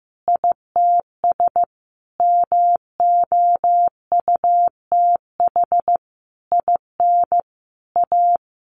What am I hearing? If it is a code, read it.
Morse code, heard as a keyed tone, about 15 wpm, 700 Hz. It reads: ITS MOUTH IN A
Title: A Sound Of Thunder 15wpm